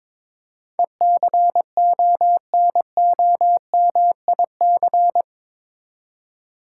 Morse code, heard as a keyed tone, about 22 wpm, 700 Hz. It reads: ECONOMIC
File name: practice / 22wpm / Eight Letter Words - 22wpm